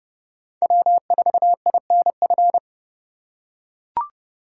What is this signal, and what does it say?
Morse code, heard as a keyed tone, about 29 wpm, 700 Hz. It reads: W4INF